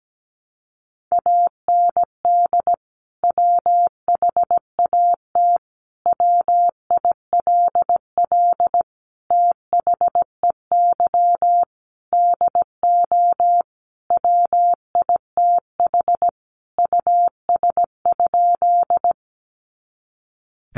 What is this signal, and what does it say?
Morse code, heard as a keyed tone, about 17 wpm, 700 Hz. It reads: AND WHAT WILL THEY DO WITH US?
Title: Sentences from Top 200 Words 17wpm